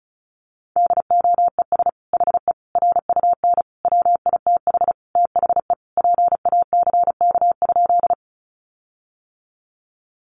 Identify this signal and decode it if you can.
Morse code, heard as a keyed tone, about 34 wpm, 700 Hz. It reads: DOES HE RUN WITH THE PACK?
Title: Sentences from Top 1300 Words 35wpm